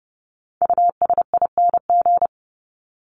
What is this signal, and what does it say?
Morse code, heard as a keyed tone, about 30 wpm, 700 Hz. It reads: USING
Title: ING Suffix - Top 100 Words with Base Words 30wpm